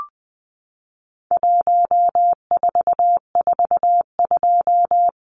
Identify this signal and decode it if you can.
Morse code, heard as a keyed tone, about 20 wpm, 700 Hz. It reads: 1442
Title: Numbers - 4-Digit 20wpm